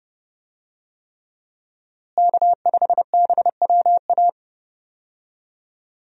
Morse code, heard as a keyed tone, about 31 wpm, 700 Hz. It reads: K5BWA
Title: USA Call Signs - 5 Characters - Repeated 3x - 30wpm